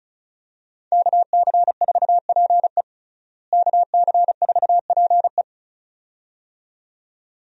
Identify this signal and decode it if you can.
Morse code, heard as a keyed tone, about 34 wpm, 700 Hz. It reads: KC4PE KC4PE
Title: USA Call Signs - 5 Characters - Repeated 2x - 35wpm